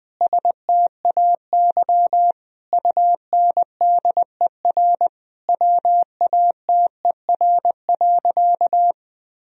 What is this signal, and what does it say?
Morse code, heard as a keyed tone, about 20 wpm, 700 Hz. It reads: STAY UNDER WATER.
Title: Sentences from Top 1000 Words - Review Part 2 - 20wpm